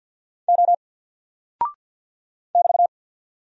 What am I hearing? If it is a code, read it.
Morse code, heard as a keyed tone, about 49 wpm, 700 Hz. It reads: O <BT>
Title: Single Character (Letter:Number:Punctuation:Prosign) Alt 50wpm